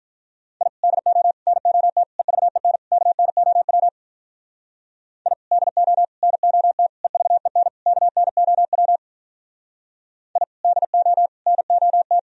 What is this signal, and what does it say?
Morse code, heard as a keyed tone, about 46 wpm, 700 Hz. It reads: I DO NOT EVEN KNOW I DO NOT EVEN KNOW I DO NOT